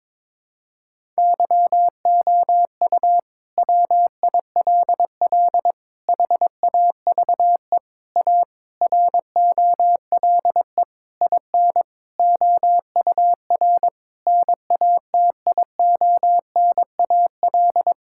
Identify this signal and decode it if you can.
Morse code, heard as a keyed tone, about 22 wpm, 700 Hz. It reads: YOU WILL HAVE A ROLE IN OUR NATIONAL
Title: Sentences from Top 1000 Words - Review Part 6 - 22wpm